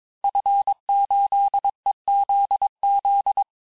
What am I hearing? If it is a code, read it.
Morse code, heard as a keyed tone, about 22 wpm, 800 Hz. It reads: F8EZZ
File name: cours-du-mercredi / audios / cours-854-liste-fin-QSO-vitesse-21-mpm